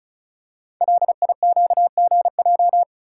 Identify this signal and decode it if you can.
Morse code, heard as a keyed tone, about 36 wpm, 700 Hz. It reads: LIQGJ
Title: Code Groups - 5 Letters - 35wpm